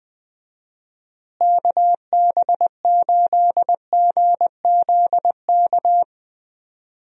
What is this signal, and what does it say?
Morse code, heard as a keyed tone, about 21 wpm, 700 Hz. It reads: KB8GZK
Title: USA Call Signs - 6 Characters - Repeated 2x - 20wpm